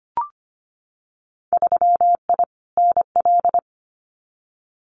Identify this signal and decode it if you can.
Morse code, heard as a keyed tone, about 25 wpm, 700 Hz. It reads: 3I NL
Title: Winter Field Day Exchanges 25wpm